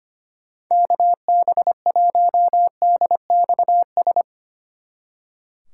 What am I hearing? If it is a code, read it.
Morse code, heard as a keyed tone, about 32 wpm, 700 Hz. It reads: KB1DXS